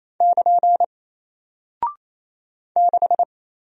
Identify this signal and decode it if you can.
Morse code, heard as a keyed tone, about 27 wpm, 700 Hz. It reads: <KN> 6
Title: Single Character (Letter:Number:Punctuation:Prosign) Alt 28wpm